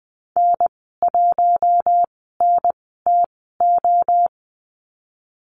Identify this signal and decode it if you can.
Morse code, heard as a keyed tone, about 20 wpm, spaced as for 14 wpm, 700 Hz. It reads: N1NTO
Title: 042 Lesson 8g - Number 1 - Call Signs 20wpm